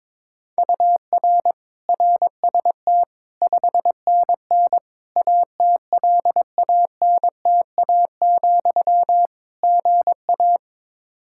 Morse code, heard as a keyed tone, about 22 wpm, 700 Hz. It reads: UR RST 5NN ATLANTA, GA
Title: Practice QSO - Exchange 1 - Beginning - 22wpm